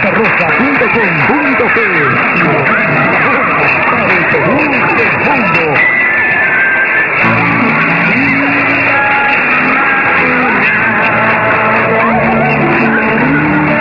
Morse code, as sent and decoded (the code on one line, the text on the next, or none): none